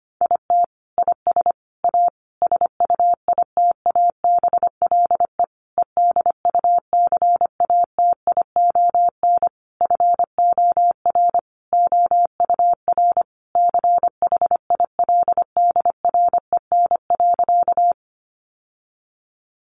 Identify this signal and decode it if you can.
Morse code, heard as a keyed tone, about 25 wpm, 700 Hz. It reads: IT IS A SUITABLE EDUCATION FOR OUR CHILDREN.
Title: Sentences from Top 1400 Words 25wpm